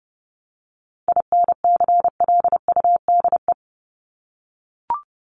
Morse code, heard as a keyed tone, about 30 wpm, 700 Hz. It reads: INCLUDE
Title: ED Suffix - Top 100 Words with Base Words 30wpm